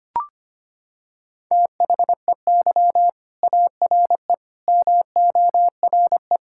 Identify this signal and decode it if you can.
Morse code, heard as a keyed tone, about 25 wpm, 700 Hz. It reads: THEY ARE MORE
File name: practice / 25wpm / Sentences with 4-Word Phrases - 25wpm